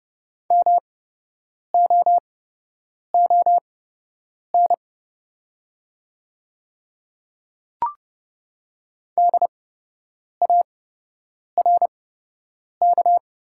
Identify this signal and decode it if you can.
Morse code, heard as a keyed tone, about 30 wpm, spaced as for 8 wpm, 700 Hz. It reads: MOON DARK
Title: 8x Character Spacing - Top 500 Words - 30wpm